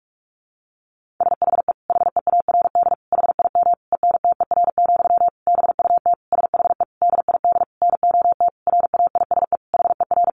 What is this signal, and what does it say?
Morse code, heard as a keyed tone, about 44 wpm, 700 Hz. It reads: SHE HEARD HIM ENTER, BUT SHE DID NOT RAISE HER
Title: Sentences from Top 1400 Words 45wpm